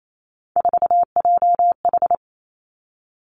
Morse code, heard as a keyed tone, about 28 wpm, 700 Hz. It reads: 4JH
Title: Three Challenging Letter-Number Combinations 28wpm